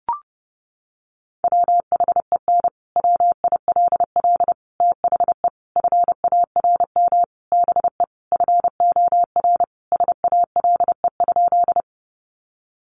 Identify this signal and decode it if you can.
Morse code, heard as a keyed tone, about 30 wpm, 700 Hz. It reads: WHEN WILL THE FARM BE FOR SALE?
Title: Sentences from Top 1400 Words 30wpm